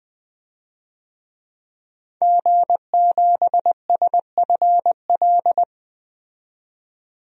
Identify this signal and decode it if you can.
Morse code, heard as a keyed tone, about 20 wpm, 700 Hz. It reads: G7SFL
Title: DXCC-100 Call Signs - 5 Characters - 20wpm